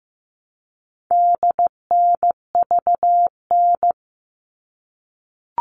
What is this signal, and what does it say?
Morse code, heard as a keyed tone, about 15 wpm, 700 Hz. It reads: DNVN
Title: All Cut Numbers - 4-Digit 15wpm